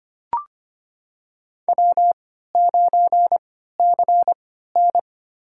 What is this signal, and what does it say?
Morse code, heard as a keyed tone, about 25 wpm, spaced as for 14 wpm, 700 Hz. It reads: W9CN
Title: USA Call Signs - 4 Characters - 3x Character Spacing - 25wpm